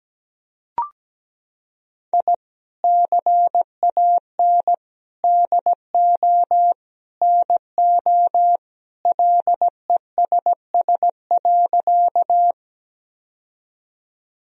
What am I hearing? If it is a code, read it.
Morse code, heard as a keyed tone, about 17 wpm, 700 Hz. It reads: I CAN DO NO LESS.
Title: Sentences from Top 1000 Words - Review Part 2 - 17wpm